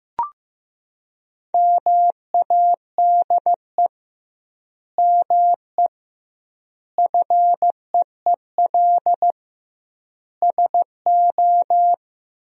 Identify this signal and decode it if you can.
Morse code, heard as a keyed tone, about 16 wpm, 700 Hz. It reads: MADE ME FEEL SO
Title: 2x Word-Spacing - Sentences from Top 400 Words - 15wpm